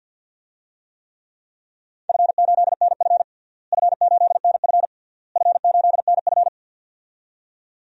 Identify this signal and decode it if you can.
Morse code, heard as a keyed tone, about 53 wpm, 700 Hz. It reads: F8NF F8NF F8NF
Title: DXCC-100 Call Signs - 4 Characters - Repeated 3x - 50wpm